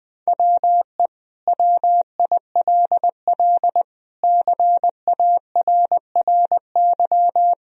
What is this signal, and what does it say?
Morse code, heard as a keyed tone, about 20 wpm, 700 Hz. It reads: WE WILL CARRY
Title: Sentences from Top 1000 Words - Review Part 1 - 20wpm